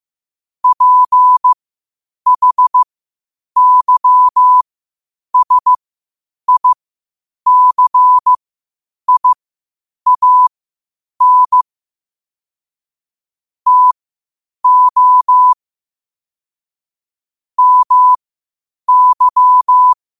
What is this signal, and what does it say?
Morse code, heard as a keyed tone, about 15 wpm, spaced as for 8 wpm, 1000 Hz. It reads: PHYSICIAN TO MY
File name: qotdfiles / 8wpm